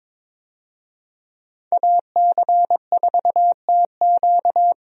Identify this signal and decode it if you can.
Morse code, heard as a keyed tone, about 21 wpm, 700 Hz. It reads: AC4TQ
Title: USA Call Signs - 5 Characters - 22wpm